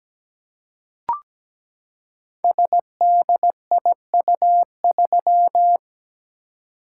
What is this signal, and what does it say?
Morse code, heard as a keyed tone, about 17 wpm, 700 Hz. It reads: SDIU3
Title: Code Groups - 5 Letter:Numbers - 17wpm